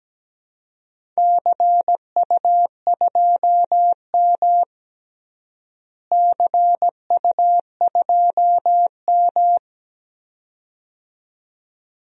Morse code, heard as a keyed tone, about 17 wpm, 700 Hz. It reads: CU2M CU2M
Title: DXCC-100 Call Signs - 4 Characters - Repeated 2x - 17wpm